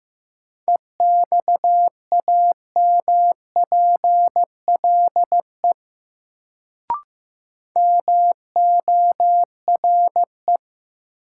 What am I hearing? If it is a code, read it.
Morse code, heard as a keyed tone, about 15 wpm, 700 Hz. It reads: EXAMPLE MORE
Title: Sets of 2 Words - New Word Drill - 15wpm